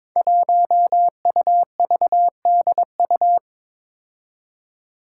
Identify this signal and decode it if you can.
Morse code, heard as a keyed tone, about 22 wpm, 700 Hz. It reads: 1UVDU
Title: Five Challenging Letter-Number Combinations 22wpm